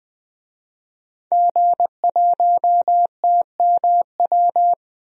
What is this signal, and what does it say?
Morse code, heard as a keyed tone, about 20 wpm, 700 Hz. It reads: G1TMW
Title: DXCC-100 Call Signs - 5 Characters - 20wpm